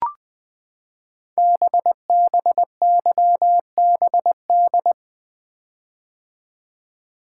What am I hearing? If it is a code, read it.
Morse code, heard as a keyed tone, about 20 wpm, 700 Hz. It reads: BBYBD
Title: Code Groups - 5 Letters - 20wpm